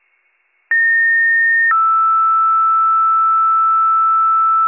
Two-tone page, heard as a one-second tone, then a three-second tone, about 1850 then 1320 Hz, with noise. It is heard from 0.7 to 4.7 s.